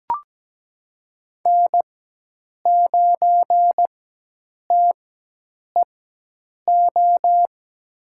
Morse code, heard as a keyed tone, about 17 wpm, spaced as for 8 wpm, 700 Hz. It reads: N9TEO
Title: USA Call Signs - 5 Characters - 4x Character Spacing - 17wpm